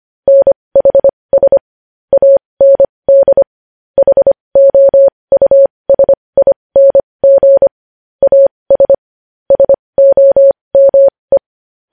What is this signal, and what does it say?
Morse code, heard as a keyed tone, about 25 wpm, spaced as for 20 wpm, 550 Hz. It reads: NHS AND HOUSING AS SOME
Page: News Headlines in Morse Code at 20 WPM